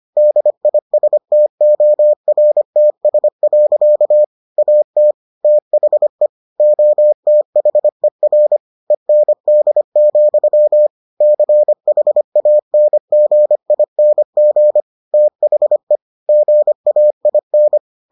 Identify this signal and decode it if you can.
Morse code, heard as a keyed tone, about 25 wpm, 600 Hz. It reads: DISTORTS. AT THE OTHER END, CHANGING THE GAIN